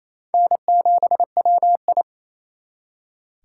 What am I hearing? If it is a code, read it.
Morse code, heard as a keyed tone, about 35 wpm, 700 Hz. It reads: N7WI